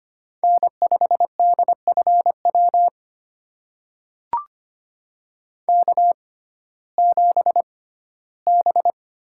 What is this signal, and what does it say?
Morse code, heard as a keyed tone, about 25 wpm, spaced as for 15 wpm, 700 Hz. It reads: N5DFW K7B
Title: USA Call Signs - 5 Characters - 6x Character Spacing - 25wpm